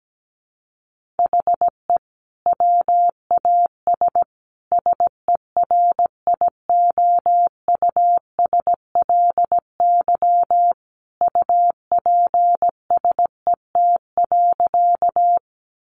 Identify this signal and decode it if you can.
Morse code, heard as a keyed tone, about 17 wpm, 700 Hz. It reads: HE WAS SERIOUSLY UPSET.
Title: Sentences from Top 1200 Words 17wpm